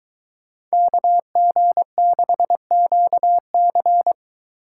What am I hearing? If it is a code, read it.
Morse code, heard as a keyed tone, about 23 wpm, 700 Hz. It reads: KG6QC